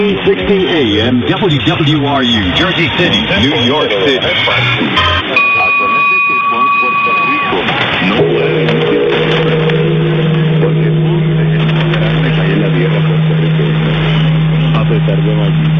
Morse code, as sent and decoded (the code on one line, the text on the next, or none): none